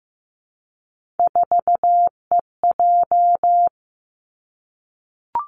Morse code, heard as a keyed tone, about 15 wpm, 700 Hz. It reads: 4EJ
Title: Three Challenging Letter-Number Combinations 15wpm